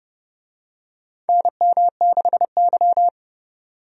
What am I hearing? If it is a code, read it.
Morse code, heard as a keyed tone, about 30 wpm, 700 Hz. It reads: NM6Y
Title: USA Call Signs - 4 Characters - 30wpm